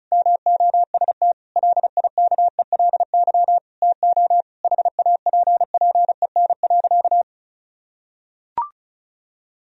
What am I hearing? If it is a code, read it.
Morse code, heard as a keyed tone, about 34 wpm, 700 Hz. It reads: MOST LIKELY TO HAPPEN.
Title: Sentences from Top 1000 Words - Review Part 4 - 35wpm